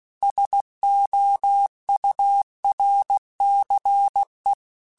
Morse code, heard as a keyed tone, about 16 wpm, 800 Hz. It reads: SOURCE
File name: Technical Subjects / Technical Subjects-mp3s / XC @15